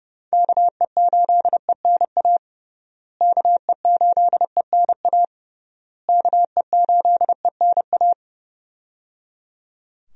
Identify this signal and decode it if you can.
Morse code, heard as a keyed tone, about 30 wpm, 700 Hz. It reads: KE8ENA KE8ENA KE8ENA
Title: USA Call Signs - 6 Characters - Repeated 3x - 30wpm